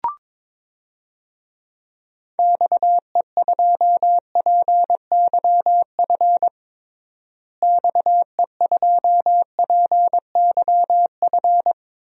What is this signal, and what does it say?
Morse code, heard as a keyed tone, about 21 wpm, 700 Hz. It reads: XE2PYF XE2PYF
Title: DXCC-100 Call Signs - 6 Characters - Repeated 2x - 22wpm